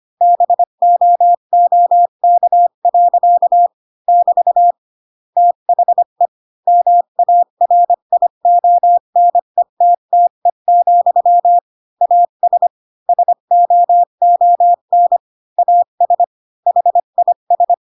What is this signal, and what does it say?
Morse code, heard as a keyed tone, about 25 wpm, 700 Hz. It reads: BOOK. <BT> THE MARIONETTE, AS SOON AS HIS
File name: Pinocchio / Pinocchio ch8 20 25 ik1hge